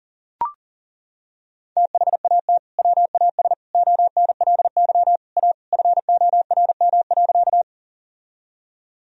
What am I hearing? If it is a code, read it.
Morse code, heard as a keyed tone, about 40 wpm, 700 Hz. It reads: THAT WAS ONLY A FORM.
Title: Sentences from Top 1000 Words - Review Part 1 - 40wpm